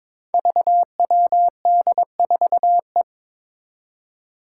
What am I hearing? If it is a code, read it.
Morse code, heard as a keyed tone, about 22 wpm, 700 Hz. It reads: VWD4E
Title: Five Challenging Letter-Number Combinations 22wpm